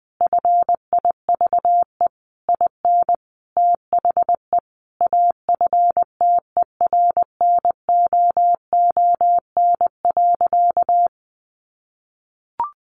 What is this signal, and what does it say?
Morse code, heard as a keyed tone, about 20 wpm, 700 Hz. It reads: FIVE IN THE AFTERNOON.
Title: Sentences from Top 1500 Words 20wpm